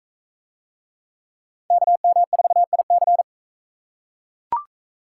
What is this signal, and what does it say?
Morse code, heard as a keyed tone, about 38 wpm, 700 Hz. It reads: KM4IC